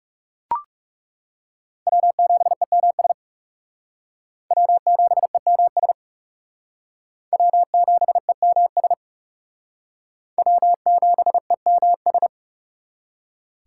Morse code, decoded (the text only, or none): W7EMS W7EMS W7EMS W7EMS